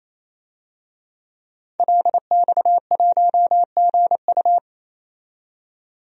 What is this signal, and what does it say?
Morse code, heard as a keyed tone, about 28 wpm, 700 Hz. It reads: LX1GU
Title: DXCC-100 Call Signs - 5 Characters - 28wpm